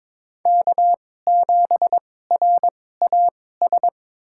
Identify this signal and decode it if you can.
Morse code, heard as a keyed tone, about 22 wpm, spaced as for 16 wpm, 700 Hz. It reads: K7RAS
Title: USA Call Signs - 5 Characters - 2x Character Spacing - 22wpm